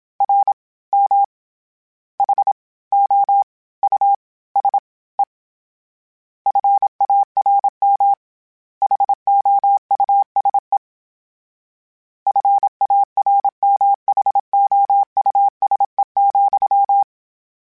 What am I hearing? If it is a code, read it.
Morse code, heard as a keyed tone, about 27 wpm, 800 Hz. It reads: R M H O U S E FARM HOUSE FARMHOUSE,
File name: General / General-mp3s / Houses @25